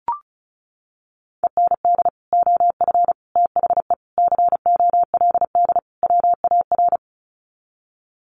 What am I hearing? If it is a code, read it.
Morse code, heard as a keyed tone, about 34 wpm, 700 Hz. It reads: END OF THE COLD WAR
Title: Sets of 5 Words Encore 35wpm